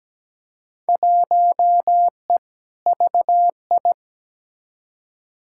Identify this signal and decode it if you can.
Morse code, heard as a keyed tone, about 17 wpm, 700 Hz. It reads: 1E VI
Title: Field Day Exchanges Encore - 17wpm